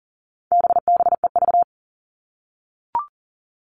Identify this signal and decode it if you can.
Morse code, heard as a keyed tone, about 40 wpm, 700 Hz. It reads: BBEV
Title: All Cut Numbers - 4-Digit 40wpm